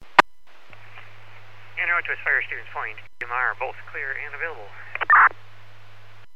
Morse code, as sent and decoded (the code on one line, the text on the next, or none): none